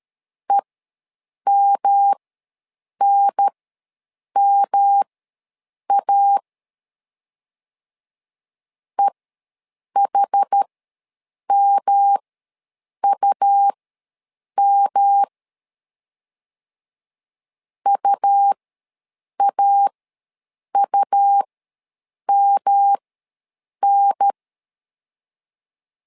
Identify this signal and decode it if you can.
Morse code, heard as a keyed tone, about 13 wpm, spaced as for 7 wpm, 800 Hz. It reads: EMNMA EHMUM UAUMN
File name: LEZIONI / 3 MWRO / 3 M ws 12wpm 1745